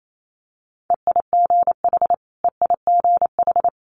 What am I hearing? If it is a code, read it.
Morse code, heard as a keyed tone, about 28 wpm, 700 Hz. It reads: EIGH EIGH
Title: Multi-letter Phonemes Repeated 3x 28wpm